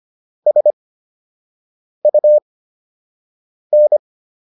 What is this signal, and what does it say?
Morse code, heard as a keyed tone, about 25 wpm, spaced as for 6 wpm, 600 Hz. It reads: SUN